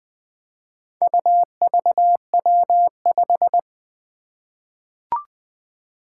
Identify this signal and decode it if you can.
Morse code, heard as a keyed tone, about 20 wpm, 700 Hz. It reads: UVW5
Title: Four Challenging Letter-Number Combinations 20wpm